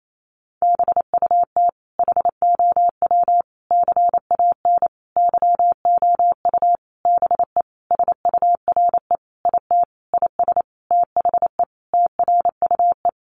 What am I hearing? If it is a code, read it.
Morse code, heard as a keyed tone, about 28 wpm, 700 Hz. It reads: BUT HOW CAN YOU BE SURE IT IS THE TRUE
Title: Sentences from Top 1400 Words 28wpm